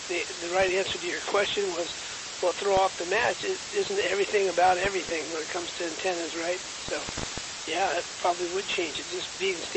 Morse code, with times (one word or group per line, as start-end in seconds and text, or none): none